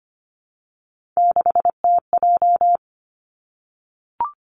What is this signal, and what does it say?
Morse code, heard as a keyed tone, about 25 wpm, 700 Hz. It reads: 6TJ
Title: Three Challenging Letter-Number Combinations 25wpm